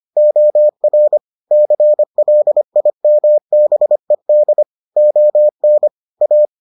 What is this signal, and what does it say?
Morse code, heard as a keyed tone, about 25 wpm, 600 Hz. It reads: OR CLIMBED ON A